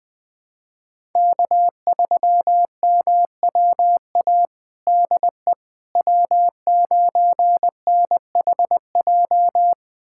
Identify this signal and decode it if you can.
Morse code, heard as a keyed tone, about 20 wpm, 700 Hz. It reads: K3MWA DE W9NHJ
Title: USA Callsign DE Callsign - 5 Characters - 20wpm